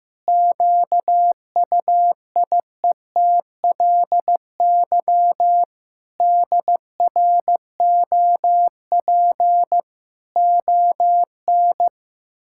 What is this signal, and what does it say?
Morse code, heard as a keyed tone, about 16 wpm, 700 Hz. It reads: QUIETLY DROP ON
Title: Pangram Challenge v2 - 15wpm